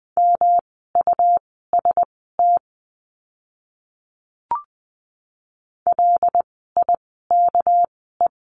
Morse code, heard as a keyed tone, about 20 wpm, spaced as for 15 wpm, 700 Hz. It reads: MUST LIKE
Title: 237 Lesson 38f - Letter q - Words 20wpm